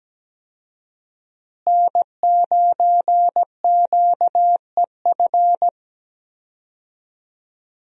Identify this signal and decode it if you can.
Morse code, heard as a keyed tone, about 17 wpm, 700 Hz. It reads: N9QEF
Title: USA Call Signs - 5 Characters - Repeated 2x - 17wpm